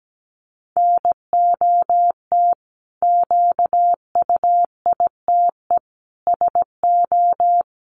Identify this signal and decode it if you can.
Morse code, heard as a keyed tone, about 17 wpm, 700 Hz. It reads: NOT QUITE SO
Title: Sentences from Top 700 Words 17wpm